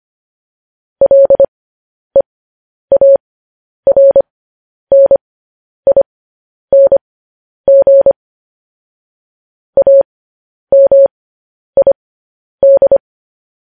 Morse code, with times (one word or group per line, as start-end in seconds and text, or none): 1.0-8.1 LEARNING
9.8-13.0 AMID